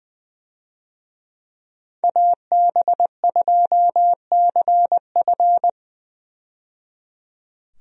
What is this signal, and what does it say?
Morse code, heard as a keyed tone, about 20 wpm, 700 Hz. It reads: AB2CF
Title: USA Call Signs - 5 Characters - 20wpm